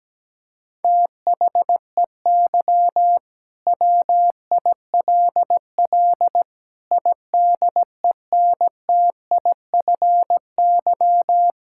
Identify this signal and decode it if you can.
Morse code, heard as a keyed tone, about 17 wpm, 700 Hz. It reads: THEY WILL IDENTIFY
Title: Sentences from Top 1000 Words - Review Part 6 - 17wpm